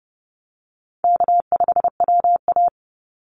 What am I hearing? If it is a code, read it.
Morse code, heard as a keyed tone, about 30 wpm, 700 Hz. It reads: K5WA
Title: Easy Call Signs Repeated 3x - 30wpm